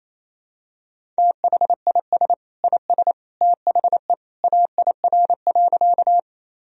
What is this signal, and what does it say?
Morse code, heard as a keyed tone, about 27 wpm, 700 Hz. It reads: THIS IS THE AIR.
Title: Sentences from Top 1000 Words - Review Part 1 - 28wpm